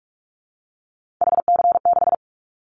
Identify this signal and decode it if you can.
Morse code, heard as a keyed tone, about 46 wpm, 700 Hz. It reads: HC6
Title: Callsign Prefixes for Top 100 DXCC - 45wpm